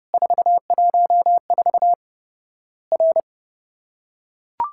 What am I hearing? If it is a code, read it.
Morse code, heard as a keyed tone, about 30 wpm, 700 Hz. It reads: 414 R
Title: Worked All Europe DX Contest - QTC Exchange - 30wpm